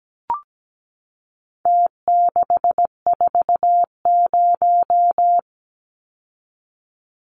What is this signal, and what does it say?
Morse code, heard as a keyed tone, about 17 wpm, 700 Hz. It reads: T640
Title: Numbers - Contest Style - 4-Digit 17wpm